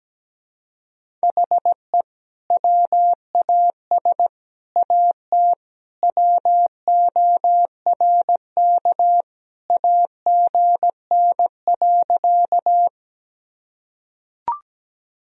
Morse code, heard as a keyed tone, about 17 wpm, 700 Hz. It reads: HE WAS AT WORK AGN.
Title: Sentences from Top 200 Words and Abbreviations 17wpm